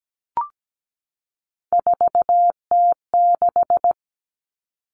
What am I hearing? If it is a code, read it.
Morse code, heard as a keyed tone, about 17 wpm, 700 Hz. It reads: 4T6
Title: All Cut Numbers - 3-Digit 17wpm